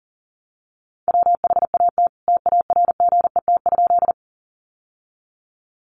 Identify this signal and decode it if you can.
Morse code, heard as a keyed tone, about 40 wpm, 700 Hz. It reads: WHAT TARGET?
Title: Sentences from Top 1200 Words 40wpm